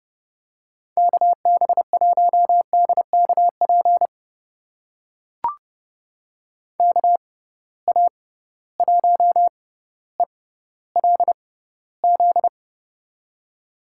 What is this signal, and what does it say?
Morse code, heard as a keyed tone, about 30 wpm, spaced as for 15 wpm, 700 Hz. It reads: KB1DKP KA1ELZ
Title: USA Call Signs - 6 Characters - 6x Character Spacing - 30wpm